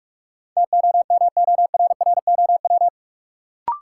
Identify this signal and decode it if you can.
Morse code, heard as a keyed tone, about 44 wpm, 700 Hz. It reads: TOMORROW
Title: Sentences from Top 1100 Words - New Word Drill Alt - 45wpm